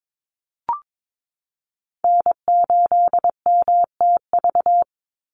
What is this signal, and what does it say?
Morse code, heard as a keyed tone, about 22 wpm, 700 Hz. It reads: N8MTV
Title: Call Signs - 22wpm